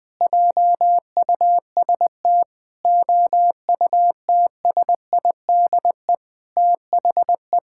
Morse code, heard as a keyed tone, about 20 wpm, 700 Hz. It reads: JUST OUTSIDE THE
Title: Sentences from Top 1000 Words - Review Part 6 - 20wpm